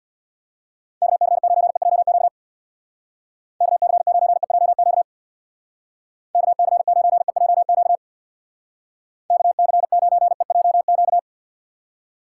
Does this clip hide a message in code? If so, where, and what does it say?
Yes: Morse code from 1.0 to 11.2 s, heard as a keyed tone, about 64 wpm, 700 Hz. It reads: KC9EJQ KC9EJQ KC9EJQ KC9EJQ